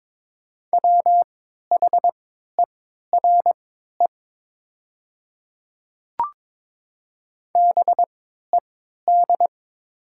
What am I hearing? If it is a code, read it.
Morse code, heard as a keyed tone, about 22 wpm, spaced as for 12 wpm, 700 Hz. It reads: WHERE BED
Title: 3x Character Spacing - Top 500 Words - 22wpm